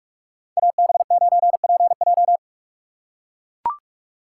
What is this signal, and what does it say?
Morse code, heard as a keyed tone, about 44 wpm, 700 Hz. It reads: AB9PJ